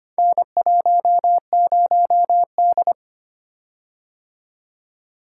Morse code, heard as a keyed tone, about 25 wpm, 700 Hz. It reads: N10D
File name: practice / 25wpm / Four Letter-Number Combinations - 25wpm